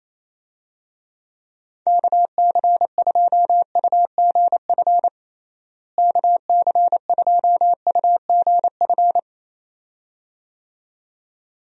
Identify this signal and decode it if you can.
Morse code, heard as a keyed tone, about 27 wpm, 700 Hz. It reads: KC2UGF KC2UGF
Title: USA Call Signs - 6 Characters - Repeated 2x - 28wpm